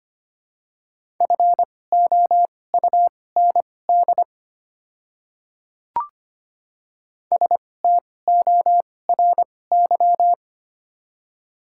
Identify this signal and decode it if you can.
Morse code, heard as a keyed tone, about 25 wpm, spaced as for 18 wpm, 700 Hz. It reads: FOUND STORY
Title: 2x Character Spacing - Top 300 Words - 25wpm